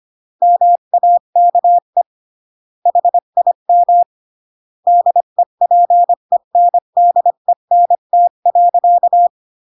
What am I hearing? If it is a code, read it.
Morse code, heard as a keyed tone, about 25 wpm, 700 Hz. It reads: MAKE HIM DEPENDENT.